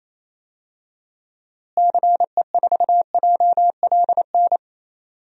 Morse code, heard as a keyed tone, about 28 wpm, 700 Hz. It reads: CE4JLN
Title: DXCC-100 Call Signs - 6 Characters - Repeated 3x - 28wpm